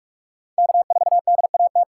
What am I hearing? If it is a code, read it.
Morse code, heard as a keyed tone, about 46 wpm, 700 Hz. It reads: K 4 D A T
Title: USA Call Signs - 5 Characters - 6x Character Spacing - 45wpm